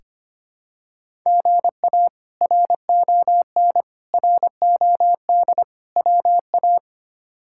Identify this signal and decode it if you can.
Morse code, heard as a keyed tone, about 25 wpm, 700 Hz. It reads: GA RON ROD WA
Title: K1USN SST QSO Practice - 25wpm